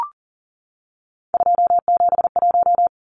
Morse code, heard as a keyed tone, about 40 wpm, 700 Hz. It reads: 271
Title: Numbers - 3-Digit 40wpm